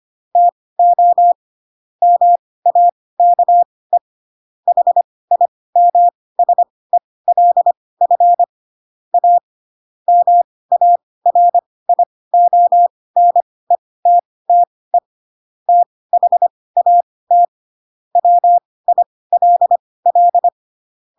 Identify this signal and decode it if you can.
Morse code, heard as a keyed tone, about 25 wpm, spaced as for 18 wpm, 700 Hz. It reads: TO MAKE HIMSELF A MARIONETTE THAT WILL
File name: Pinocchio / Pinocchio ch2 12 25 ik1hge